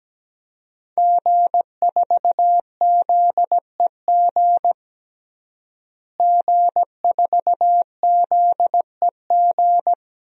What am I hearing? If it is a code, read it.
Morse code, heard as a keyed tone, about 17 wpm, 700 Hz. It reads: G4ZEG G4ZEG
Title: DXCC-100 Call Signs - 5 Characters - Repeated 3x - 17wpm